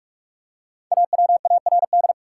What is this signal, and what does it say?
Morse code, heard as a keyed tone, about 44 wpm, 700 Hz. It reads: AWARD
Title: Five Letter Words - 45wpm